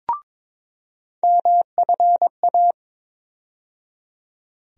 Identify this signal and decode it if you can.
Morse code, heard as a keyed tone, about 22 wpm, 700 Hz. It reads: MFA
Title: USA Call Sign Postfixes - 3 Letters 22wpm